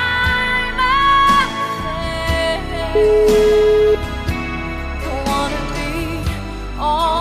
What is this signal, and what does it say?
Ringback tone (Europe, 425 Hz), with noise.